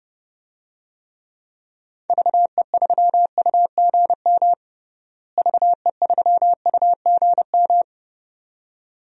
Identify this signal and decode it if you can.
Morse code, heard as a keyed tone, about 29 wpm, 700 Hz. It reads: VE3UGM VE3UGM
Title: DXCC-100 Call Signs - 6 Characters - Repeated 2x - 30wpm